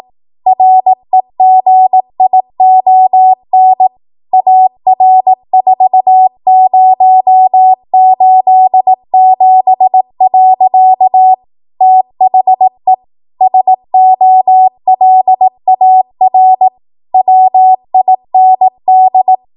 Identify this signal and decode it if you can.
Morse code, heard as a keyed tone, about 18 wpm, 750 Hz. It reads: REGION AR4087. THE SOLAR WIND
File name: Morse / 18WPM